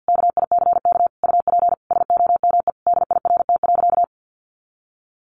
Morse code, heard as a keyed tone, about 50 wpm, 700 Hz. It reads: KICK UP SOME DIRT.
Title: A Sound Of Thunder 50wpm